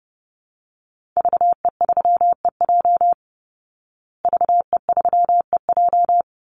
Morse code, heard as a keyed tone, about 30 wpm, 700 Hz. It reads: VE3EJ VE3EJ
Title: Call Signs Repeated 3x - 30wpm